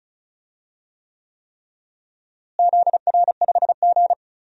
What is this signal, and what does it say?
Morse code, heard as a keyed tone, about 36 wpm, 700 Hz. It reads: ZR5G
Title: DXCC-100 Call Signs - 4 Characters - Repeated 3x - 35wpm